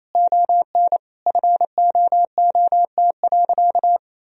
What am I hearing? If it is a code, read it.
Morse code, heard as a keyed tone, about 28 wpm, 700 Hz. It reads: ON FOOT.
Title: Sentences from Top 1000 Words - Review Part 3 - 28wpm